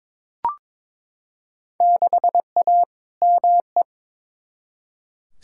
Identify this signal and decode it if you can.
Morse code, heard as a keyed tone, about 21 wpm, 700 Hz. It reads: 6A ME
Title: Field Day Exchanges - 22wpm